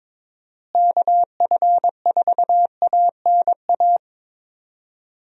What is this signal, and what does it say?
Morse code, heard as a keyed tone, about 22 wpm, 700 Hz. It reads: KF4ANA
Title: USA Call Signs - 6 Characters - Repeated 3x - 22wpm